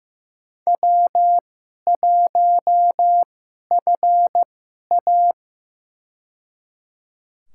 Morse code, heard as a keyed tone, about 16 wpm, spaced as for 11 wpm, 700 Hz. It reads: W1FA
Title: USA Call Signs - 2x Character Spacing - 15wpm